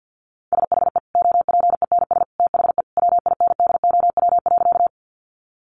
Morse code, heard as a keyed tone, about 49 wpm, 700 Hz. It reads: SHE OPENS THE WINDOW.
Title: Additional Easy Phrases 50wpm